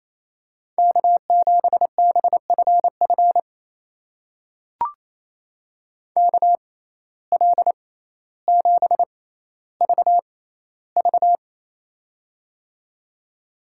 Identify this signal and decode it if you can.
Morse code, heard as a keyed tone, about 28 wpm, spaced as for 14 wpm, 700 Hz. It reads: K7BFF KL7VV
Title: USA Call Signs - 5 Characters - 6x Character Spacing - 28wpm